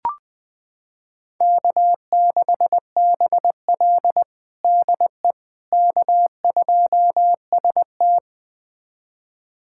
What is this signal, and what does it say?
Morse code, heard as a keyed tone, about 20 wpm, 700 Hz. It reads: K6BL DE K2ST
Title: USA Callsign DE Callsign - 4 Characters - 20wpm